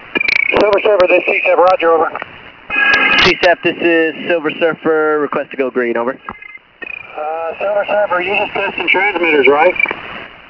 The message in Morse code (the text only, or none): none